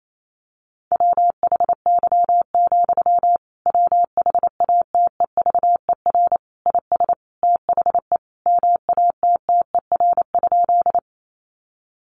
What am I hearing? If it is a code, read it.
Morse code, heard as a keyed tone, about 28 wpm, 700 Hz. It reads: WHY, WHATEVER IS THE MATTER?
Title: Sentences from Top 1100 Words 28wpm